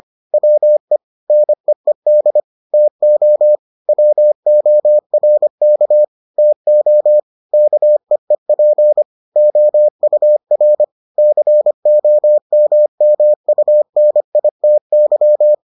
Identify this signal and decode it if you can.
Morse code, heard as a keyed tone, about 25 wpm, 600 Hz. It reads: WE NEED TO WORK TO KEEP OUR COMMUNITY